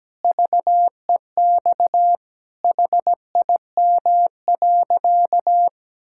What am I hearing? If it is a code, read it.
Morse code, heard as a keyed tone, about 17 wpm, 700 Hz. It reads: VEX HIM.
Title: Pangram Challenge v2 - 17wpm